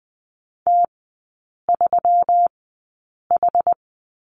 Morse code, heard as a keyed tone, about 20 wpm, spaced as for 8 wpm, 700 Hz. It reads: T3H
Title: 138 Lesson 23e - Letter f - Three Characters 20wpm